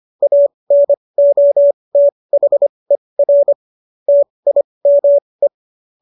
Morse code, heard as a keyed tone, about 25 wpm, spaced as for 20 wpm, 570 Hz. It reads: ANOTHER TIME